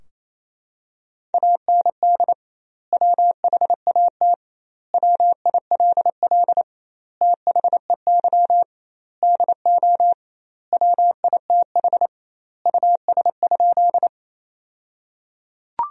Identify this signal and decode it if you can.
Morse code, heard as a keyed tone, about 27 wpm, 700 Hz. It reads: AND WHAT WILL THEY DO WITH US?
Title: 2x Word-Spacing - Sentences from Top 200 Words - 28wpm